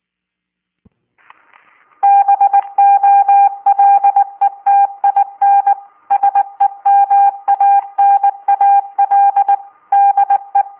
Morse code, heard as a keyed tone, about 20 wpm, 800 Hz. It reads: BOLETIN SEMANAL DE